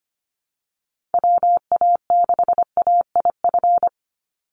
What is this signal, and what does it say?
Morse code, heard as a keyed tone, about 25 wpm, 700 Hz. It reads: WA6AIF
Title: USA Call Signs 25wpm